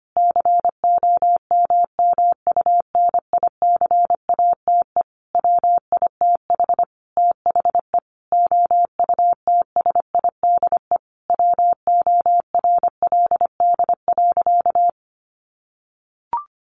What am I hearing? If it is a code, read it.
Morse code, heard as a keyed tone, about 25 wpm, 700 Hz. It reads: COMMUNICATE WITH THE OUTSIDE WORLD.
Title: Sentences from Top 1300 Words 25wpm